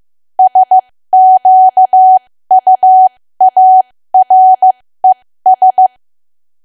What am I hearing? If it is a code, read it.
Morse code, heard as a keyed tone, about 15 wpm, spaced as for 13 wpm, 750 Hz. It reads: SQUARES
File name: cwa / W1AW-Level2 / 13WPM